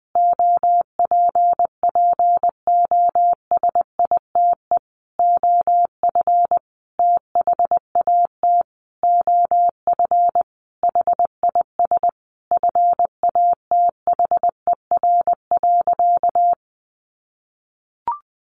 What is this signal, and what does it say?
Morse code, heard as a keyed tone, about 19 wpm, 700 Hz. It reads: OPPOSITE OF THAT OF HIS FATHER.
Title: Sentences from Top 1300 Words 20wpm